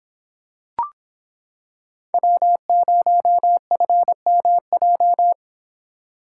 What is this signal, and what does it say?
Morse code, heard as a keyed tone, about 26 wpm, 700 Hz. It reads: W0FMJ